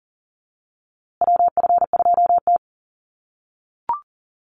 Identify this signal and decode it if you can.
Morse code, heard as a keyed tone, about 40 wpm, 700 Hz. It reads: WF2T
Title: USA Call Signs Repeated 2x 40wpm